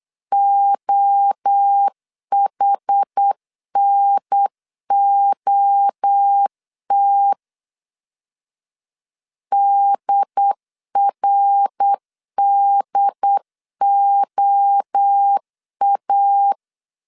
Morse code, heard as a keyed tone, about 8 wpm, 800 Hz. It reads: OHNOT DRDOA